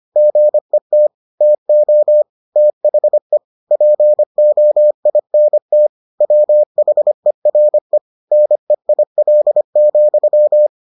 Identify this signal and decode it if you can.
Morse code, heard as a keyed tone, about 25 wpm, 600 Hz. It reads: GET TO THE POINT WHERE NEIL,